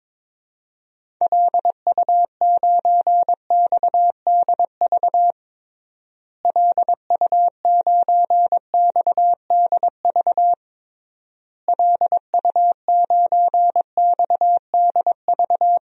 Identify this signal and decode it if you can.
Morse code, heard as a keyed tone, about 22 wpm, 700 Hz. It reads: LU9XDV LU9XDV LU9XDV
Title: DXCC-100 Call Signs - 6 Characters - Repeated 3x - 22wpm